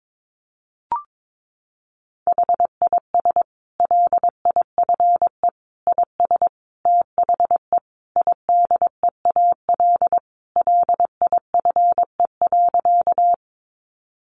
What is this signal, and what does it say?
Morse code, heard as a keyed tone, about 22 wpm, 700 Hz. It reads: HIS LIFE IS THE IDEAL LIFE.
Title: Sentences from Top 1400 Words 22wpm